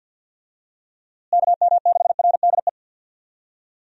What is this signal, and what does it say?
Morse code, heard as a keyed tone, about 47 wpm, 700 Hz. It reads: KM6RDE
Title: USA Call Signs - 6 Characters - Repeated 2x - 50wpm